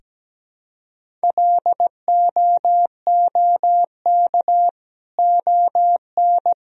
Morse code, heard as a keyed tone, about 17 wpm, 700 Hz. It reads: LOOK ON
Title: Sentences from Top 1000 Words - Review Part 1 - 17wpm